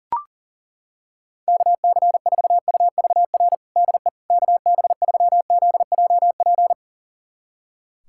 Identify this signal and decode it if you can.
Morse code, heard as a keyed tone, about 40 wpm, 700 Hz. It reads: KC4UVR DE KB3ZJP
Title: USA Callsign DE Callsign 40wpm